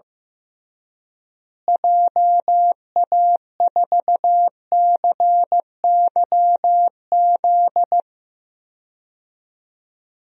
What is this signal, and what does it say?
Morse code, heard as a keyed tone, about 16 wpm, 700 Hz. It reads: JA4CYZ
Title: DXCC-100 Call Signs - 6 Characters - Repeated 3x - 15wpm